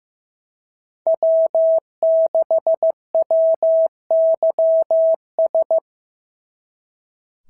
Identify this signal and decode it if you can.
Morse code, heard as a keyed tone, about 15 wpm, 690 Hz. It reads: W6WYS
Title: SOTA QSO Practice 15wpm